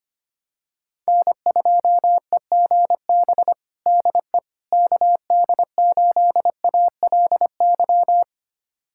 Morse code, heard as a keyed tone, about 25 wpm, 700 Hz. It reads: N2EGB DE KD8ALY
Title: USA Callsign DE Callsign 25wpm